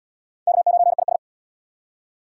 Morse code, heard as a keyed tone, about 71 wpm, 690 Hz. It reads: K9IS